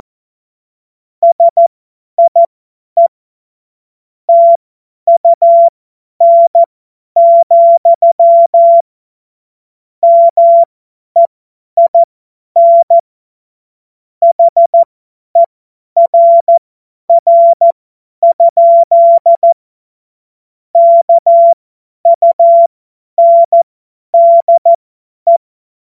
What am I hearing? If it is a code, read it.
Morse code, heard as a keyed tone, about 14 wpm, spaced as for 10 wpm, 680 Hz. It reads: SIE TUN, MEIN HERR? KUNDE
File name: Morsekurs / Witze / Witz 16 Januar 2024 50 F70